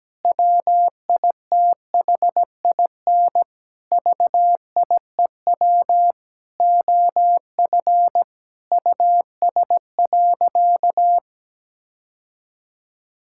Morse code, heard as a keyed tone, about 17 wpm, 700 Hz. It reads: WITHIN VIEW OF US.
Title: Sentences from Top 1000 Words - Review Part 6 - 17wpm